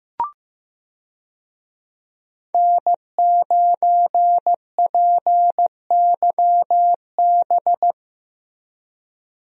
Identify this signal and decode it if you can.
Morse code, heard as a keyed tone, about 15 wpm, 700 Hz. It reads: N9PYB
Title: USA Call Signs - 5 Characters - Repeated 3x - 15wpm